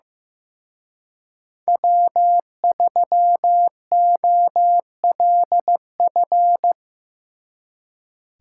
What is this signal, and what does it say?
Morse code, heard as a keyed tone, about 17 wpm, 700 Hz. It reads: W3OLF
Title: Speed Racing - USA 1x3 Call Signs 15wpm